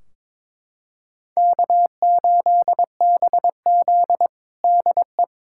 Morse code, heard as a keyed tone, about 22 wpm, 700 Hz. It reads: K8BZ DE
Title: Practice QSO - Exchange 4 - Ending - 22wpm